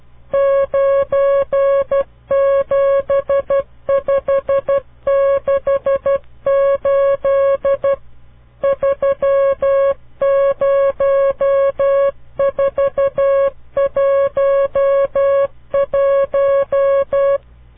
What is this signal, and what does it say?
Morse code, heard as a keyed tone, about 12 wpm, 550 Hz. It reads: 97568 30411
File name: CW Learning Programs / farnsworth / unit 5